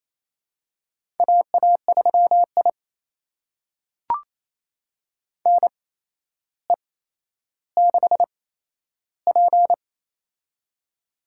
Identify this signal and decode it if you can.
Morse code, heard as a keyed tone, about 27 wpm, spaced as for 12 wpm, 700 Hz. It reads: AA3I NE6P
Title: USA Call Signs - 4 Characters - 8x Character Spacing - 28wpm